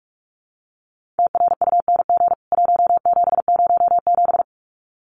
Morse code, heard as a keyed tone, about 44 wpm, 700 Hz. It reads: TRUNG 1707
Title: CWT Exchange 45wpm